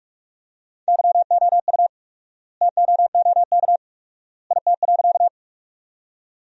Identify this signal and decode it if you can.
Morse code, heard as a keyed tone, about 47 wpm, 700 Hz. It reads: YOU TOOK IT.
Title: 4x Word-Spacing - Sentences from Top 300 Words - 45wpm